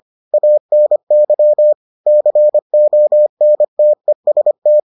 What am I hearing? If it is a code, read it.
Morse code, heard as a keyed tone, about 25 wpm, 600 Hz. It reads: ANY CONTEST